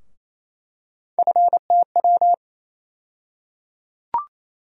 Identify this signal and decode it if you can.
Morse code, heard as a keyed tone, about 28 wpm, 700 Hz. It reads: FTW
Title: USA Call Sign Postfixes - 3 Letters 28wpm